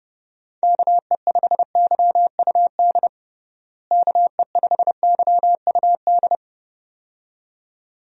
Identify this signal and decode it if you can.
Morse code, heard as a keyed tone, about 30 wpm, 700 Hz. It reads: KE5YUD KE5YUD
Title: USA Call Signs - 6 Characters - Repeated 2x - 30wpm